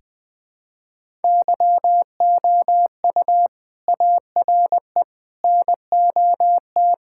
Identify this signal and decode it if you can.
Morse code, heard as a keyed tone, about 20 wpm, 700 Hz. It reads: YOU ARE NOT
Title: Sentences with 7-Word Phrases - 20wpm